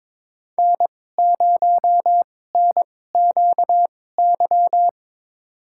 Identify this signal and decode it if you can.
Morse code, heard as a keyed tone, about 21 wpm, spaced as for 16 wpm, 700 Hz. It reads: N0NQY
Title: USA Call Signs - 5 Characters - 2x Character Spacing - 22wpm